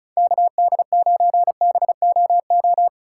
Morse code, heard as a keyed tone, about 36 wpm, 700 Hz. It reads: KD9BOO
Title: USA Call Signs - 6 Characters - 4x Character Spacing - 35wpm